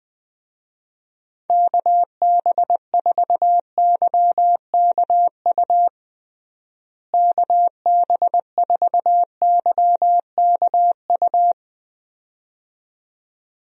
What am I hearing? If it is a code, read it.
Morse code, heard as a keyed tone, about 20 wpm, 700 Hz. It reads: KB4YKU KB4YKU
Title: USA Call Signs - 6 Characters - Repeated 2x - 20wpm